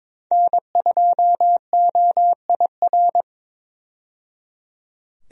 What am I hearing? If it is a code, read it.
Morse code, heard as a keyed tone, about 22 wpm, 700 Hz. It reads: N2OIR
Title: USA Call Signs - 5 Characters - 22wpm